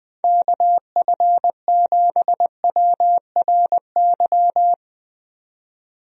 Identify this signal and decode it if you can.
Morse code, heard as a keyed tone, about 21 wpm, 700 Hz. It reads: KF7WRY
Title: USA Call Signs - 6 Characters - 20wpm